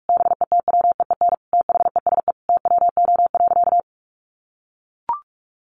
Morse code, heard as a keyed tone, about 44 wpm, 700 Hz. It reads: BETWEEN THESE TWO.
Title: Sentences from Top 1200 Words 45wpm